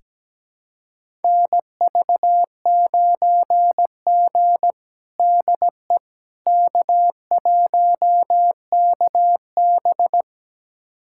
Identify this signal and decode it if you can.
Morse code, heard as a keyed tone, about 17 wpm, 700 Hz. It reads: NV9G DE K1KB
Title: USA Callsign DE Callsign - 4 Characters - 17wpm